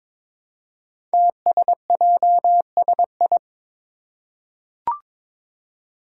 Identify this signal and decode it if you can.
Morse code, heard as a keyed tone, about 22 wpm, 700 Hz. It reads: TSJSI
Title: Five Challenging Letter-Number Combinations 22wpm